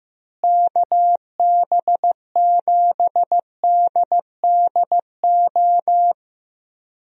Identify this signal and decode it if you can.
Morse code, heard as a keyed tone, about 16 wpm, 700 Hz. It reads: KB7DDO
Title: USA Call Signs - 6 Characters - Repeated 2x - 15wpm